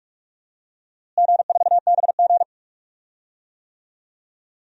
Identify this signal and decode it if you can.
Morse code, heard as a keyed tone, about 41 wpm, 700 Hz. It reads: G4BG
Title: DXCC-100 Call Signs - 4 Characters - Repeated 2x - 45wpm